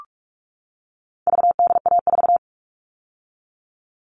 Morse code, heard as a keyed tone, about 44 wpm, 700 Hz. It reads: VDA4
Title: All Cut Numbers - 4-Digit 45wpm